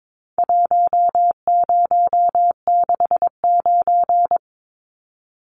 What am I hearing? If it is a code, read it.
Morse code, heard as a keyed tone, about 22 wpm, 700 Hz. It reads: 1069
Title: Numbers - Contest Style - 4-Digit 22wpm